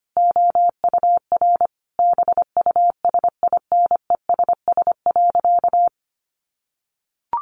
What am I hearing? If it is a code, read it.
Morse code, heard as a keyed tone, about 25 wpm, 700 Hz. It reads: OUR BUSINESS.
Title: Sentences from Top 1400 Words 25wpm